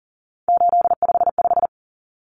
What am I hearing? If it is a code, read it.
Morse code, heard as a keyed tone, about 40 wpm, 700 Hz. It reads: 855
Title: Numbers - 3-Digit 40wpm